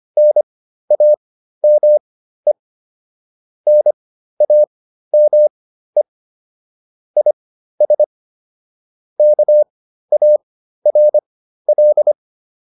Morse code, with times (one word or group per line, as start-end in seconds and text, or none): 0.2-2.5 NAME
3.7-6.0 NAME
7.2-8.0 IS
9.2-12.1 KARL